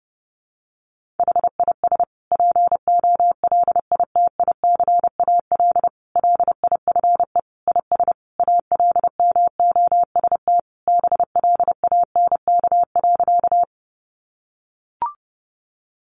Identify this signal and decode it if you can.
Morse code, heard as a keyed tone, about 30 wpm, 700 Hz. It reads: HIS POLITICAL LIFE IS ALMOST BLANK.
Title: Sentences from Top 1500 Words 30wpm